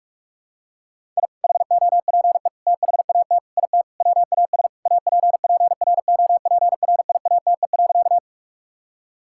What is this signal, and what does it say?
Morse code, heard as a keyed tone, about 45 wpm, 700 Hz. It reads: I HOPE THAT IT WAS APPROPRIATE.
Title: Sentences from Top 1000 Words - Review Part 5 - 45wpm